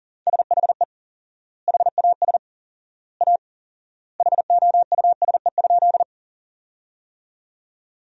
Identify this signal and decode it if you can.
Morse code, heard as a keyed tone, about 37 wpm, 700 Hz. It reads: SHE HAS A HOUSE?
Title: 4x Word-Spacing - Sentences from Top 200 Words - 40wpm